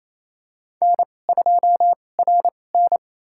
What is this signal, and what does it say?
Morse code, heard as a keyed tone, about 28 wpm, spaced as for 20 wpm, 700 Hz. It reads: N2RN
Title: USA Call Signs - 2x Character Spacing - 28wpm